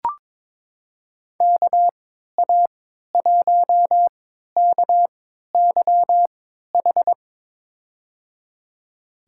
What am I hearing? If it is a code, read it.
Morse code, heard as a keyed tone, about 22 wpm, spaced as for 13 wpm, 700 Hz. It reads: KA1KYH
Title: USA Call Signs - 3x Character Spacing - 22wpm